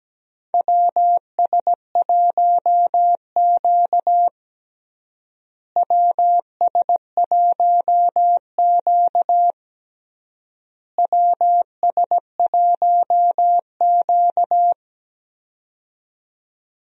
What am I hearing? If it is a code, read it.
Morse code, heard as a keyed tone, about 17 wpm, 700 Hz. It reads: WS1Q WS1Q WS1Q
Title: USA Call Signs - 4 Characters - Repeated 3x - 17wpm